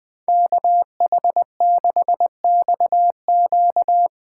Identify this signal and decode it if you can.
Morse code, heard as a keyed tone, about 21 wpm, 700 Hz. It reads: KH6XQ
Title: Speed Racing - USA Call Signs 20wpm